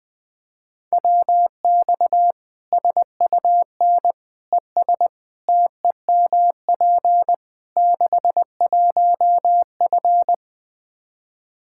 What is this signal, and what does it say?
Morse code, heard as a keyed tone, about 21 wpm, 700 Hz. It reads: WX SUN ES TEMP 61F
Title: Practice QSO - Exchange 2 - Info - 20wpm